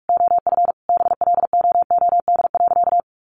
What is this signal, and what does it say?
Morse code, heard as a keyed tone, about 44 wpm, 700 Hz. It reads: OF BLOOD.
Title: Sentences from Top 1100 Words 45wpm